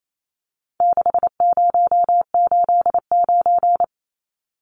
Morse code, heard as a keyed tone, about 28 wpm, 700 Hz. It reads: 6089
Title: Numbers - 4-Digit 28wpm